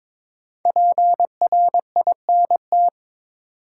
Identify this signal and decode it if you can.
Morse code, heard as a keyed tone, about 22 wpm, 700 Hz. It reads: PRINT
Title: Sentences from Top 1400 Words - New Word Drill Alt - 22wpm